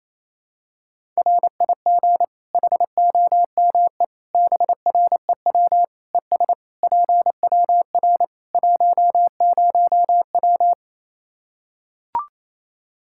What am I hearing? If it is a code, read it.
Morse code, heard as a keyed tone, about 28 wpm, 700 Hz. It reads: RIG HOME BREW ES PWR 10W
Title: Practice QSO - Exchange 2 - Info - 28wpm